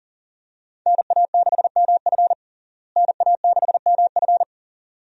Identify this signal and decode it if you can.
Morse code, heard as a keyed tone, about 40 wpm, 700 Hz. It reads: NA6MF NA6MF
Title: USA Call Signs - 5 Characters - Repeated 2x - 40wpm